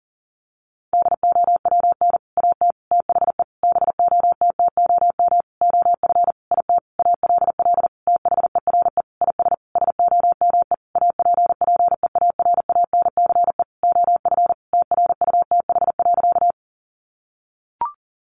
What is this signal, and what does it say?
Morse code, heard as a keyed tone, about 37 wpm, 700 Hz. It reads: DOWN AT THE BOTTOM OF IT ALL THERE IS SOME APPEARANCE OF TRUTH.
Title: Sentences from Top 1100 Words 40wpm